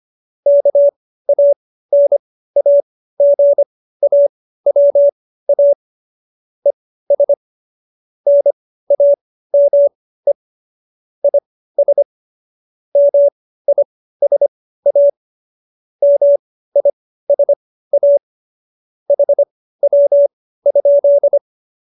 Morse code, heard as a keyed tone, about 25 wpm, spaced as for 15 wpm, 570 Hz. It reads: KANAGAWA ES NAME IS MISA MISA HW?